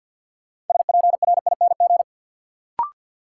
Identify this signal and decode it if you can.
Morse code, heard as a keyed tone, about 53 wpm, 700 Hz. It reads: SPRING
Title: Sentences from Top 1200 Words - New Word Drill Alt - 50wpm